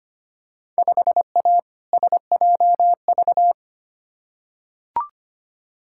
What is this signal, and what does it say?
Morse code, heard as a keyed tone, about 25 wpm, 700 Hz. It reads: 5A SJV
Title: Field Day Exchanges - 25wpm